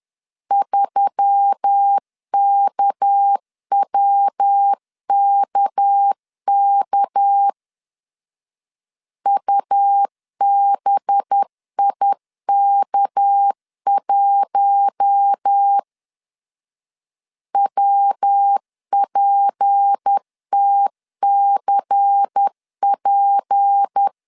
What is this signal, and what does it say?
Morse code, heard as a keyed tone, about 10 wpm, 800 Hz. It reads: 3KWKK UBIK1 WPTCP
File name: LEZIONI / 6 PKX0 / 6 PK 10wpm 2436